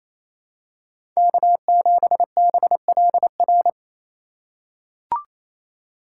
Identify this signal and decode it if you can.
Morse code, heard as a keyed tone, about 28 wpm, 700 Hz. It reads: K7BLR
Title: USA Call Signs - 5 Characters - 2x Character Spacing - 28wpm